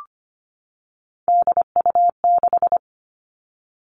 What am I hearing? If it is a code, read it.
Morse code, heard as a keyed tone, about 25 wpm, 700 Hz. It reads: DU6
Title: Three Challenging Letter-Number Combinations 25wpm